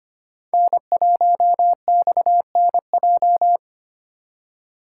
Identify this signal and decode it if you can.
Morse code, heard as a keyed tone, about 25 wpm, 700 Hz. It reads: N1XNJ
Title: USA Call Signs - 5 Characters - 3x Character Spacing - 25wpm